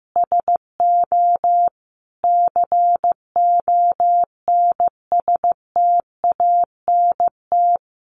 Morse code, heard as a keyed tone, about 15 wpm, 700 Hz. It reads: SO CONSTANT
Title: Sentences from Top 1500 Words 15wpm